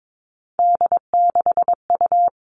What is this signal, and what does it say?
Morse code, heard as a keyed tone, about 22 wpm, 700 Hz. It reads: D6U
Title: All Cut Numbers - 3-Digit 22wpm